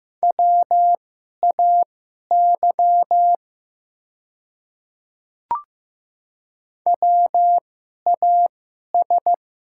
Morse code, heard as a keyed tone, about 15 wpm, spaced as for 11 wpm, 690 Hz. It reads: WAY WAS
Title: 2x Character Spacing - Top 200 Words - 15wpm